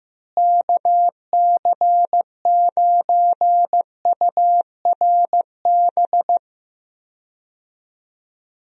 Morse code, heard as a keyed tone, about 15 wpm, 700 Hz. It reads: KC9URB
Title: USA Call Signs - 6 Characters - Repeated 2x - 15wpm